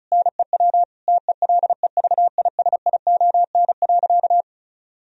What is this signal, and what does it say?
Morse code, heard as a keyed tone, about 34 wpm, 700 Hz. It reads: NEW TELEVISION.
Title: Sentences from Top 1000 Words - Review Part 4 - 35wpm